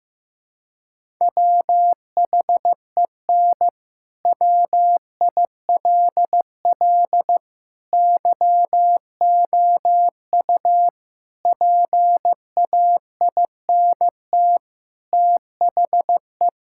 Morse code, heard as a keyed tone, about 15 wpm, 700 Hz. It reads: WHEN WILL YOU PAINT THE
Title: Sentences from Top 1000 Words - Review Part 6 - 15wpm